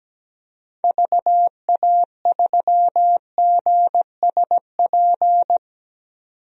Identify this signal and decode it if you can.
Morse code, heard as a keyed tone, about 17 wpm, 700 Hz. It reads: VA3GSP
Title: DXCC-100 Call Signs - 6 Characters - 17wpm